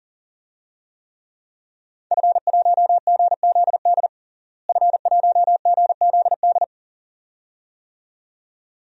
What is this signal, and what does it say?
Morse code, heard as a keyed tone, about 40 wpm, 700 Hz. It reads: F1GZD F1GZD
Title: DXCC-100 Call Signs - 5 Characters - Repeated 2x - 40wpm